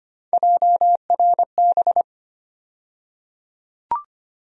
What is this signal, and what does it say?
Morse code, heard as a keyed tone, about 25 wpm, 700 Hz. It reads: JRB
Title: USA Call Sign Postfixes - 3 Letters 25wpm